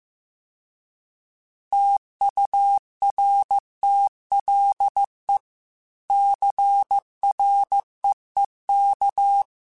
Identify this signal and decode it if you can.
Morse code, heard as a keyed tone, about 15 wpm, 800 Hz. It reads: TURTLE CREEK